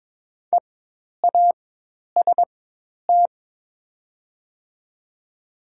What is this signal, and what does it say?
Morse code, heard as a keyed tone, about 22 wpm, spaced as for 10 wpm, 700 Hz. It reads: EAST
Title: 4x Character Spacing - Top 500 Words - 22wpm